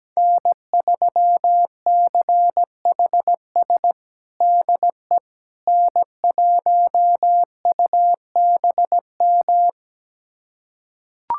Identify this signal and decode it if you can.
Morse code, heard as a keyed tone, about 17 wpm, 700 Hz. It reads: N3CHS DE N1UBM
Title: USA Callsign DE Callsign - 5 Characters - 17wpm